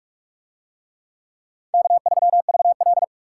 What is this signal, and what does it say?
Morse code, heard as a keyed tone, about 44 wpm, 700 Hz. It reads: K3VL